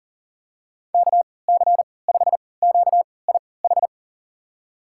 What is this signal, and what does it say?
Morse code, heard as a keyed tone, about 37 wpm, spaced as for 22 wpm, 700 Hz. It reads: KC5QIH
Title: USA Call Signs - 6 Characters - 3x Character Spacing - 40wpm